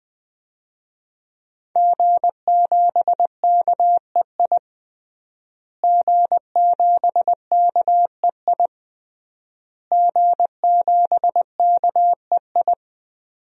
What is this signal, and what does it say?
Morse code, heard as a keyed tone, about 20 wpm, 700 Hz. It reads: G7KEI G7KEI G7KEI
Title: DXCC-100 Call Signs - 5 Characters - Repeated 3x - 20wpm